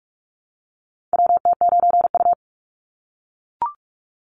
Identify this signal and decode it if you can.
Morse code, heard as a keyed tone, about 44 wpm, 700 Hz. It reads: WT9U
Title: Call Signs - 45wpm